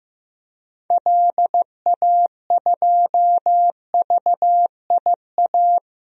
Morse code, heard as a keyed tone, about 15 wpm, 700 Hz. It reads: LA2VIA
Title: DXCC-100 Call Signs - 6 Characters - Repeated 3x - 15wpm